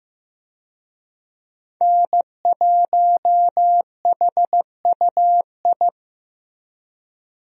Morse code, heard as a keyed tone, about 16 wpm, 700 Hz. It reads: N1HUI
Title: USA Call Signs - 5 Characters - Repeated 3x - 15wpm